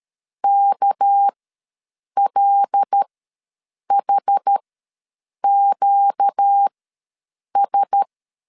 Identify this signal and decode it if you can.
Morse code, heard as a keyed tone, about 13 wpm, spaced as for 7 wpm, 800 Hz. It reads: KLHQS
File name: LEZIONI / 6 PKX0 / 6 PK ws 12wpm 3116